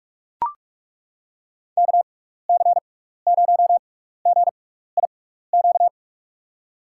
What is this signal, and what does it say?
Morse code, heard as a keyed tone, about 43 wpm, spaced as for 15 wpm, 700 Hz. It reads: KC0GIQ
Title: USA Call Signs - 6 Characters - 6x Character Spacing - 45wpm